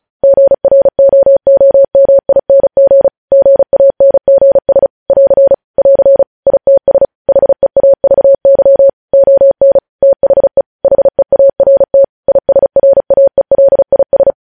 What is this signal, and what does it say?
Morse code, heard as a keyed tone, about 36 wpm, 550 Hz. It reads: GROOMING GANGS <AR> <AR> ITS HEAVY ON THE HEART ISRAELIS
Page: news headlines in morse code at 35 wpm